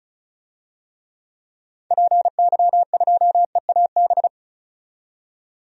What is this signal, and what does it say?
Morse code, heard as a keyed tone, about 36 wpm, 700 Hz. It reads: PY2EAB
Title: DXCC-100 Call Signs - 6 Characters - 35wpm